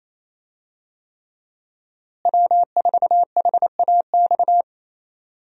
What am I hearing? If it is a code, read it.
Morse code, heard as a keyed tone, about 28 wpm, 700 Hz. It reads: W4HAX
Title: USA Call Signs - 5 Characters - Repeated 3x - 28wpm